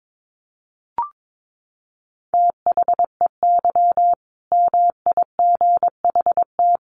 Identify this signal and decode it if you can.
Morse code, heard as a keyed tone, about 22 wpm, 700 Hz. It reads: THEY MIGHT
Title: Sentences from Top 1100 Words 22wpm